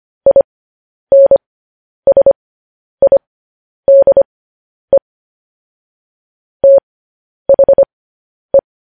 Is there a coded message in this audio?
Yes: Morse code from 0.3 to 8.6 s, heard as a keyed tone, about 25 wpm, spaced as for 10 wpm, 550 Hz. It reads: INSIDE THE